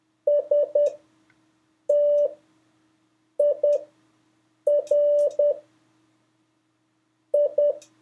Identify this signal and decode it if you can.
Morse code, heard as a keyed tone, about 10 wpm, spaced as for 6 wpm, 580 Hz. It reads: STIR I
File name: 3of5-words-2